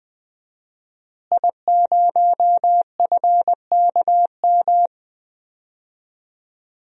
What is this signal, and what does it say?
Morse code, heard as a keyed tone, about 20 wpm, 700 Hz. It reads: I0FKM
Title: DXCC-100 Call Signs - 5 Characters - Repeated 3x - 20wpm